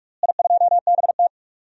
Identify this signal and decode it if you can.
Morse code, heard as a keyed tone, about 45 wpm, 700 Hz. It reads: I2BT
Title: DXCC-100 Call Signs - 4 Characters - Repeated 3x - 45wpm